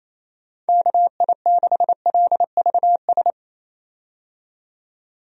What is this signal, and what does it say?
Morse code, heard as a keyed tone, about 28 wpm, 700 Hz. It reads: KI6LVS
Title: USA Call Signs - 6 Characters - 28wpm